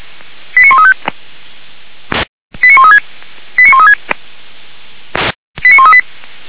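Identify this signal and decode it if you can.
Five-tone selective call (ZVEI), heard as quick runs of five tones, tones of about 70 ms. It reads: 89135 89136 89136 89138
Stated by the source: ZVEI five-tone call received by SDR, from the Feuerwehr Orlamünde audio server.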